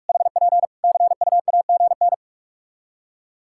Morse code, heard as a keyed tone, about 46 wpm, 700 Hz. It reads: HP CUAGN
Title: Practice QSO - Exchange 4 - Ending - 45wpm